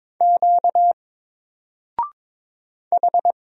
Morse code, heard as a keyed tone, about 22 wpm, 700 Hz. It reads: Q H
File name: Single-Letter-Number-22wpm